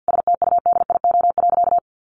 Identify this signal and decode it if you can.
Morse code, heard as a keyed tone, about 50 wpm, 700 Hz. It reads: STUDIO.
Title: Sentences from Top 1200 Words 50wpm